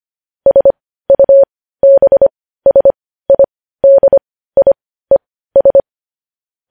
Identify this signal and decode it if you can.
Morse code, heard as a keyed tone, about 25 wpm, spaced as for 15 wpm, 550 Hz. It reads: SUBSIDIES
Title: News Headlines at 15